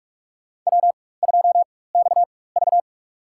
Morse code, heard as a keyed tone, about 46 wpm, spaced as for 21 wpm, 700 Hz. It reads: W2XV
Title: USA Call Signs - 4x Character Spacing - 45wpm